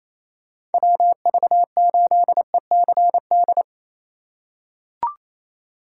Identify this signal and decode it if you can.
Morse code, heard as a keyed tone, about 27 wpm, 700 Hz. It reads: WV8ECD
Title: USA Call Signs - 6 Characters - 3x Character Spacing - 28wpm